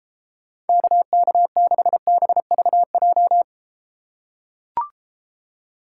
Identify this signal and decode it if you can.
Morse code, heard as a keyed tone, about 33 wpm, 700 Hz. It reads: KK6BVJ